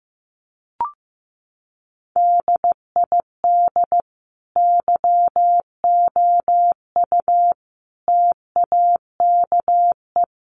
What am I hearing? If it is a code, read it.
Morse code, heard as a keyed tone, about 15 wpm, 700 Hz. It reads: DID YOU TAKE
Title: Sentences from Top 1400 Words 15wpm